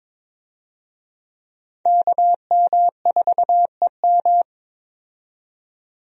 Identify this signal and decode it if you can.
Morse code, heard as a keyed tone, about 23 wpm, 700 Hz. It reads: KM4EM
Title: USA Call Signs - 5 Characters - 22wpm